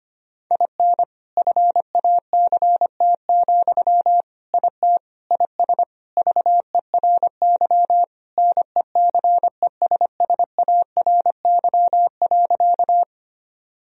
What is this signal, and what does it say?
Morse code, heard as a keyed tone, about 25 wpm, 700 Hz. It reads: IN FACT, IT IS VERY NECESSARY.
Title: Sentences from Top 1000 Words - Review Part 3 - 25wpm